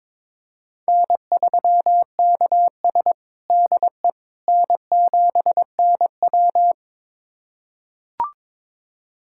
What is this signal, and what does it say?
Morse code, heard as a keyed tone, about 22 wpm, 700 Hz. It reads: N3KS DE N7NW
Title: USA Callsign DE Callsign - 4 Characters - 22wpm